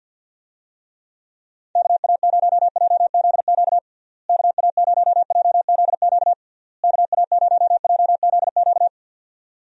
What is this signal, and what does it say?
Morse code, heard as a keyed tone, about 51 wpm, 700 Hz. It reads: KA0JZQ KA0JZQ KA0JZQ
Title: USA Call Signs - 6 Characters - Repeated 3x - 50wpm